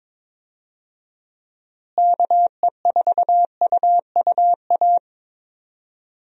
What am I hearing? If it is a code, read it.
Morse code, heard as a keyed tone, about 22 wpm, 700 Hz. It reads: KE4UUA
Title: USA Call Signs - 6 Characters - Repeated 2x - 22wpm